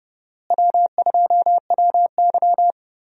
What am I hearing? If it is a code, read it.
Morse code, heard as a keyed tone, about 30 wpm, 700 Hz. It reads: W2WY
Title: USA Call Signs - 4 Characters - Repeated 3x - 30wpm